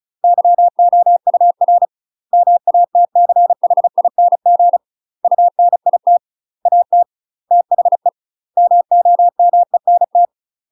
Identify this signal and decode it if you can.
Morse code, heard as a keyed tone, about 36 wpm, 700 Hz. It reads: YOUR MATCHING UNIT AT THE MOMENT
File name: UK MASTER QSOs TRAINING / 35WPM / UK MASTER QSOs 0015